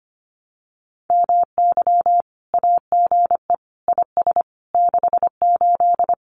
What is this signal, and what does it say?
Morse code, heard as a keyed tone, about 25 wpm, 700 Hz. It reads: MY AGE IS 68
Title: Common QSO Phrases Encore 25wpm